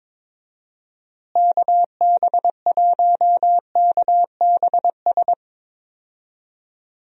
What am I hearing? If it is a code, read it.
Morse code, heard as a keyed tone, about 22 wpm, 700 Hz. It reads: KB1KBS
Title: USA Call Signs - 6 Characters - Repeated 3x - 22wpm